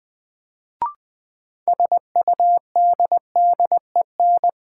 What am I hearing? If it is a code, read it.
Morse code, heard as a keyed tone, about 20 wpm, 700 Hz. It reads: SUDDEN
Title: Sets of 4 Words - New Word Drill - 20wpm